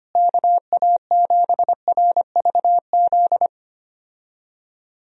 Morse code, heard as a keyed tone, about 25 wpm, 700 Hz. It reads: KA7RVZ
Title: USA Call Signs - 6 Characters - Repeated 2x - 25wpm